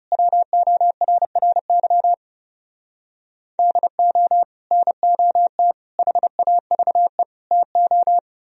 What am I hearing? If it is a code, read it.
Morse code, heard as a keyed tone, about 38 wpm, 700 Hz. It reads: WORRY DO NOT HAVE TO